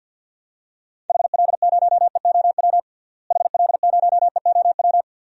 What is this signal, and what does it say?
Morse code, heard as a keyed tone, about 47 wpm, 700 Hz. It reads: HL0EOW HL0EOW
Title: DXCC-100 Call Signs - 6 Characters - Repeated 2x - 50wpm